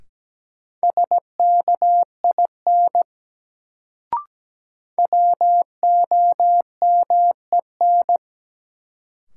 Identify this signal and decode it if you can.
Morse code, heard as a keyed tone, about 17 wpm, 700 Hz. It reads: SKIN WOMEN
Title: Sentences from Top 1300 Words - New Word Drill Alt - 17wpm